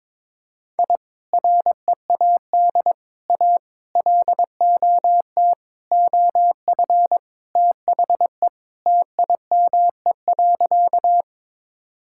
Morse code, heard as a keyed tone, about 22 wpm, 700 Hz. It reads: I READ A LOT OF THE TIME.
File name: practice / 22wpm / Sentences with 4-Word Phrases - 22wpm